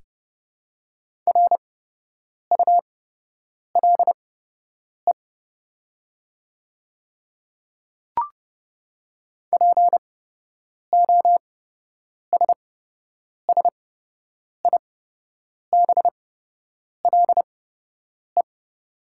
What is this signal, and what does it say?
Morse code, heard as a keyed tone, about 30 wpm, spaced as for 8 wpm, 700 Hz. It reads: RULE POSSIBLE
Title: 8x Character Spacing - Top 500 Words - 30wpm